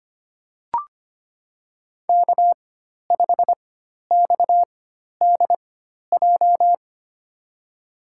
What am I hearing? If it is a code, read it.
Morse code, heard as a keyed tone, about 25 wpm, spaced as for 12 wpm, 700 Hz. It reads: K5XDJ
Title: USA Call Signs - 5 Characters - 4x Character Spacing - 25wpm